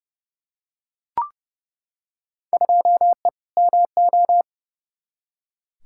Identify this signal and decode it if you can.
Morse code, heard as a keyed tone, about 32 wpm, 700 Hz. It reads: 2E MO
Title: Field Day Exchanges - 30wpm